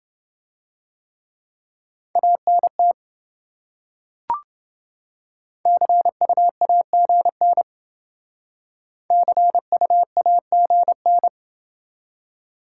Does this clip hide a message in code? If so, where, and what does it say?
Yes: Morse code from 2.2 to 11.3 s, heard as a keyed tone, about 28 wpm, 700 Hz. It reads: ANT CUAGN CUAGN